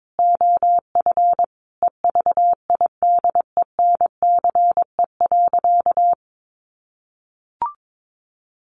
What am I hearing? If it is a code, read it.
Morse code, heard as a keyed tone, about 22 wpm, 700 Hz. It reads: OF EVIDENCE.
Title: Sentences from Top 1300 Words 22wpm